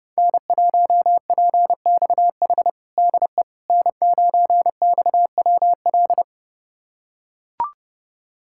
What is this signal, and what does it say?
Morse code, heard as a keyed tone, about 30 wpm, 700 Hz. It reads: N1PXH DE N9XWL
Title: USA Callsign DE Callsign - 5 Characters - 30wpm